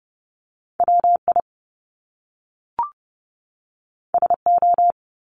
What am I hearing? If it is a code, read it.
Morse code, heard as a keyed tone, about 30 wpm, 700 Hz. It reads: WI SO
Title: Top Two Letter Combinations in Words 30wpm